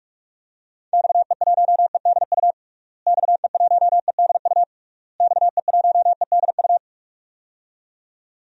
Morse code, heard as a keyed tone, about 46 wpm, 700 Hz. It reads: XE1EDU XE1EDU XE1EDU
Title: DXCC-100 Call Signs - 6 Characters - Repeated 3x - 45wpm